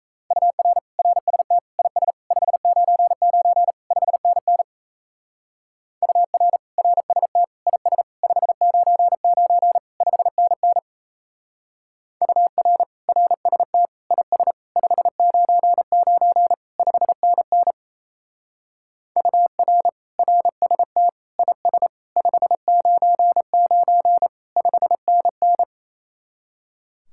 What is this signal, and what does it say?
Morse code, heard as a keyed tone, about 34 wpm, 700 Hz. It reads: UR RST IS 599 5NN UR RST IS 599 5NN UR RST IS 599 5NN UR RST IS 599 5NN
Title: Speed-Racing Common QSO Phrases - 28wpm